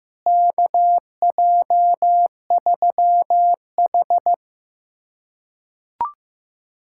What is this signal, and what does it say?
Morse code, heard as a keyed tone, about 15 wpm, 700 Hz. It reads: KJ3H
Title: USA Callsign DE Callsign - 4 Characters - 15wpm